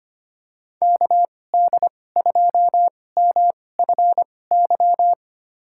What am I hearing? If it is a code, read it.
Morse code, heard as a keyed tone, about 25 wpm, spaced as for 18 wpm, 700 Hz. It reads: KD2MFY
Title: USA Call Signs - 6 Characters - 2x Character Spacing - 25wpm